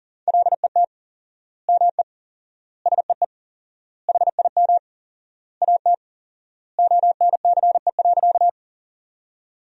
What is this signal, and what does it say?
Morse code, heard as a keyed tone, about 40 wpm, 700 Hz. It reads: LET ME SEE HIM AT ONCE.
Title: 4x Word-Spacing - Sentences from Top 300 Words - 40wpm